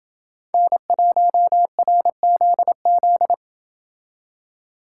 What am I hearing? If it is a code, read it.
Morse code, heard as a keyed tone, about 28 wpm, 700 Hz. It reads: N1RZZ